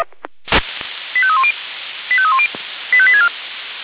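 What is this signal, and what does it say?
Five-tone selective call (ZVEI), heard as quick runs of five tones, tones of about 70 ms. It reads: 85310 85310 85864
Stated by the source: ZVEI five-tone call received by SDR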